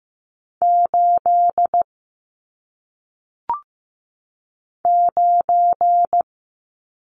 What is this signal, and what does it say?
Morse code, heard as a keyed tone, about 15 wpm, 700 Hz. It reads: 8 9
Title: Numbers - 1-Digit 15wpm